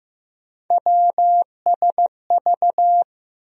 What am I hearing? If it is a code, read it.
Morse code, heard as a keyed tone, about 15 wpm, 700 Hz. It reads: WSV
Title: USA Call Sign Postfixes - 3 Letters 15wpm